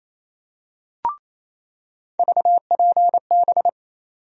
Morse code, heard as a keyed tone, about 28 wpm, 700 Hz. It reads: VPB
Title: USA Call Sign Postfixes - 3 Letters 28wpm